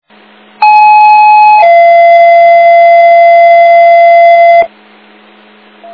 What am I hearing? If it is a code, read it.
Two-tone page, heard as a one-second tone, then a three-second tone, about 850 then 680 Hz, with noise. It is heard from 0.6 to 4.6 s.